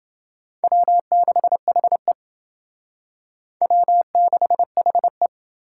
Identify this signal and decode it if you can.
Morse code, heard as a keyed tone, about 28 wpm, 700 Hz. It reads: W6HE W6HE